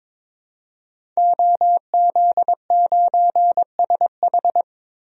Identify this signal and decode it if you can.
Morse code, heard as a keyed tone, about 22 wpm, 700 Hz. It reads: OZ9SH
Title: DXCC-100 Call Signs - 5 Characters - 22wpm